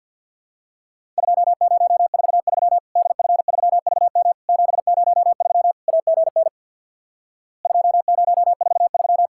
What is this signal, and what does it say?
Morse code, heard as a keyed tone, about 48 wpm, 700 Hz. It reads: 2043 DF3VM 703 AGN 2043
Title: Worked All Europe DX Contest - QTC Exchange with Repeats - 50wpm